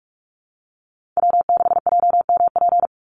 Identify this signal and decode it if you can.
Morse code, heard as a keyed tone, about 44 wpm, 700 Hz. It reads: W6JMP
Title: Call Signs - 45wpm